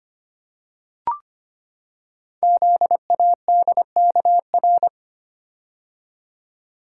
Morse code, heard as a keyed tone, about 25 wpm, 700 Hz. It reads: ZADKR
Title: Code Groups - 5 Letters - 25wpm